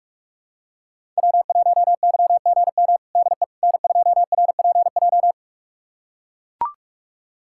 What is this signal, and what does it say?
Morse code, heard as a keyed tone, about 42 wpm, 700 Hz. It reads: W1YGM DE N2RPJ
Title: USA Callsign DE Callsign - 5 Characters - 45wpm